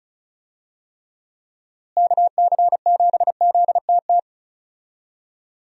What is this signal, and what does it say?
Morse code, heard as a keyed tone, about 36 wpm, 700 Hz. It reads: KC7ZTT
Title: USA Call Signs - 6 Characters - Repeated 3x - 35wpm